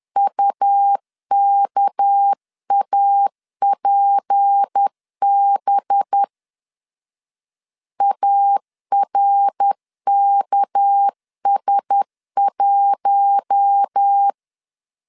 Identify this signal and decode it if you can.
Morse code, heard as a keyed tone, about 10 wpm, 800 Hz. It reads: UKAPB ARKS1
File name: CW / DET / 1 6 1500C S B 10wpm 4137